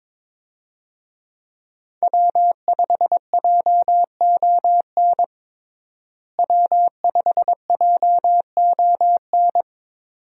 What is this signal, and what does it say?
Morse code, heard as a keyed tone, about 22 wpm, 700 Hz. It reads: W5JON W5JON
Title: USA Call Signs - 5 Characters - Repeated 2x - 22wpm